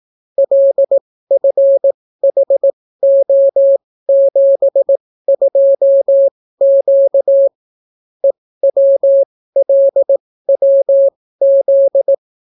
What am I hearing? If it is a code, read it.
Morse code, heard as a keyed tone, about 18 wpm, spaced as for 14 wpm, 550 Hz. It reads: LFHO72Q EWLWZ